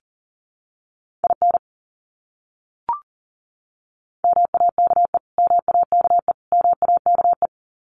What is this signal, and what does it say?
Morse code, heard as a keyed tone, about 40 wpm, 700 Hz. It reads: IN MAKE MAKE MAKE
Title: Top 300 Words Repeated 40wpm